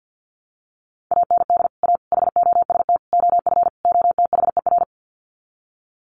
Morse code, heard as a keyed tone, about 50 wpm, 700 Hz. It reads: AND A HOST OF OTHER
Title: Sets of 5 Words Encore2 50wpm